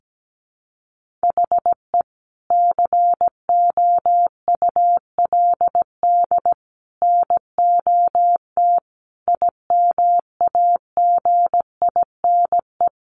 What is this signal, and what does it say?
Morse code, heard as a keyed tone, about 17 wpm, 700 Hz. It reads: HE COULD NOT IMAGINE
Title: Sentences from Top 1400 Words 17wpm